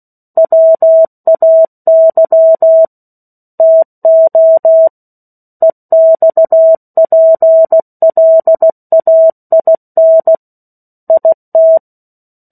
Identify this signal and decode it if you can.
Morse code, heard as a keyed tone, about 16 wpm, 650 Hz. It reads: WAY TO EXPLAIN IT
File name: 03-Chapter-03 16 WPM